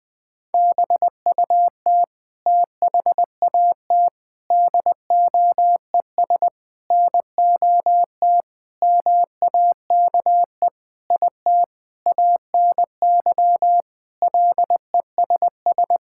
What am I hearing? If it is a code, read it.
Morse code, heard as a keyed tone, about 21 wpm, 700 Hz. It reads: BUT THAT DOES NOT MAKE IT ANY LESS
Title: Sentences with 7-Word Phrases - Spoken Beginning - 20wpm